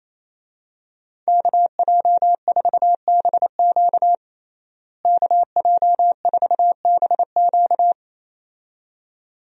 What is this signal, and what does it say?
Morse code, heard as a keyed tone, about 28 wpm, 700 Hz. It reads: KJ4BQ KJ4BQ
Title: USA Call Signs - 5 Characters - Repeated 2x - 28wpm